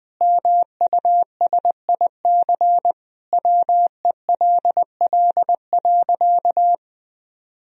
Morse code, heard as a keyed tone, about 20 wpm, 700 Hz. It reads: MUSIC WELL.
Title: Sentences from Top 1000 Words - Review Part 1 - 20wpm